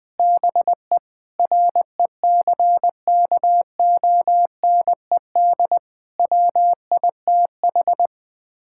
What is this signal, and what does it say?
Morse code, heard as a keyed tone, about 20 wpm, 700 Hz. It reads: BE RECKONED WITH
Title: Sets of 6 Words - 20wpm